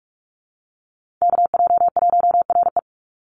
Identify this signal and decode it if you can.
Morse code, heard as a keyed tone, about 44 wpm, 700 Hz. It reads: KJ1RE
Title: Call Signs Repeated 3x - 45wpm